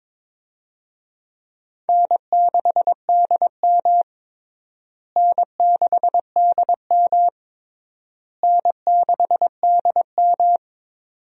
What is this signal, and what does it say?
Morse code, heard as a keyed tone, about 22 wpm, 700 Hz. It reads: N6DM N6DM N6DM
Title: USA Call Signs - 4 Characters - Repeated 3x - 22wpm